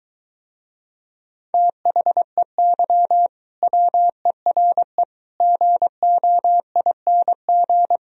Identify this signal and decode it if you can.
Morse code, heard as a keyed tone, about 23 wpm, 700 Hz. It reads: THEY WERE GOING